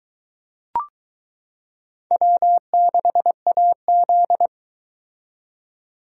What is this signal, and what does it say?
Morse code, heard as a keyed tone, about 21 wpm, 700 Hz. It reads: W6AZ